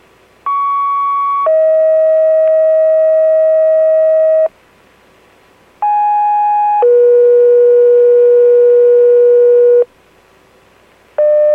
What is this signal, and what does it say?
Two-tone page, heard as a one-second tone, then a three-second tone, about 1120 then 640 Hz, about 830 then 480 Hz. It is heard from 0.5 to 9.8 s.